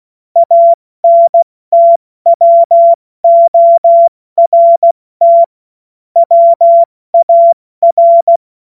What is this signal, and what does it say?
Morse code, heard as a keyed tone, about 16 wpm, 680 Hz. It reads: ANTWORT WAR
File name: Morsekurs / Witze / Witz 03. Mai 2022 70 F80